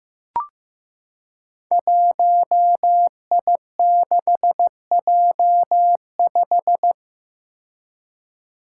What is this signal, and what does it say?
Morse code, heard as a keyed tone, about 15 wpm, 700 Hz. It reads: 1I6J5
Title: Five Challenging Letter-Number Combinations 15wpm